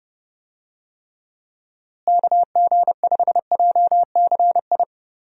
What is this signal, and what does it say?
Morse code, heard as a keyed tone, about 30 wpm, 700 Hz. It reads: KG5JCI
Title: USA Call Signs - 6 Characters - Repeated 3x - 30wpm